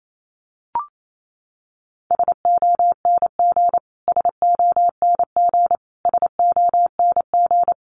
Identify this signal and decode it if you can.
Morse code, heard as a keyed tone, about 28 wpm, 700 Hz. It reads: SONG SONG SONG
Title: Top 500 Words Repeated 28wpm